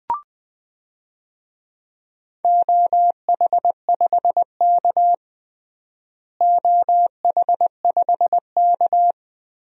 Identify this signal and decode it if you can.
Morse code, heard as a keyed tone, about 19 wpm, 700 Hz. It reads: OH5K OH5K
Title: DXCC-100 Call Signs - 4 Characters - Repeated 2x - 20wpm